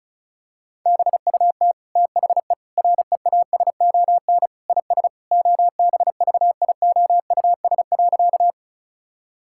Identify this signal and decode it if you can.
Morse code, heard as a keyed tone, about 34 wpm, 700 Hz. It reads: BUT THE REASON IS OBVIOUS.
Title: Sentences from Top 1600 Words 35wpm